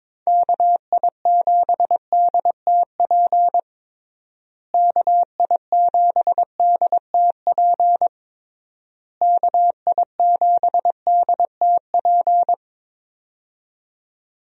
Morse code, heard as a keyed tone, about 22 wpm, 700 Hz. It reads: KI7DTP KI7DTP KI7DTP
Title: USA Call Signs - 6 Characters - Repeated 3x - 22wpm